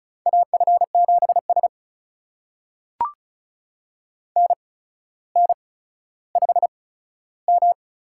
Morse code, heard as a keyed tone, about 36 wpm, spaced as for 15 wpm, 700 Hz. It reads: AF7S NN5M
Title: USA Call Signs - 4 Characters - 8x Character Spacing - 35wpm